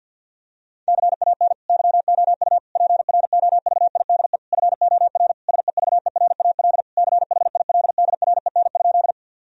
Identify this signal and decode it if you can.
Morse code, heard as a keyed tone, about 52 wpm, 700 Hz. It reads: CAN YOU PROVIDE FOR SEVERAL CHILDREN?
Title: Sentences from Top 1000 Words - Review Part 2 - 50wpm